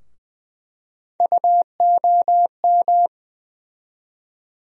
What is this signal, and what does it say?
Morse code, heard as a keyed tone, about 21 wpm, 700 Hz. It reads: UOM
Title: USA Call Sign Postfixes - 3 Letters 20wpm